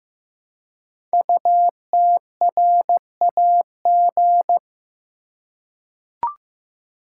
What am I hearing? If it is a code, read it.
Morse code, heard as a keyed tone, about 15 wpm, 700 Hz. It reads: UTRAG
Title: Code Groups - 5 Letter:Numbers - 15wpm